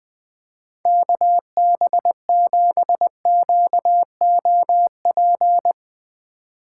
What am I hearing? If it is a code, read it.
Morse code, heard as a keyed tone, about 20 wpm, 700 Hz. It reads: KB7QOP
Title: K1USN SST QSO Practice - 20wpm